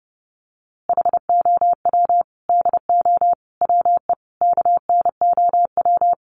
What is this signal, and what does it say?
Morse code, heard as a keyed tone, about 28 wpm, 700 Hz. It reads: HOW DO WE KNOW
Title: A Sound Of Thunder 30wpm